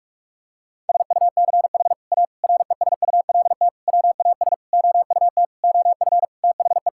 Morse code, heard as a keyed tone, about 46 wpm, 700 Hz. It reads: SUCH A RESULT WAS OUT OF THE
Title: Sentences from Top 1000 Words - Review Part 5 - 45wpm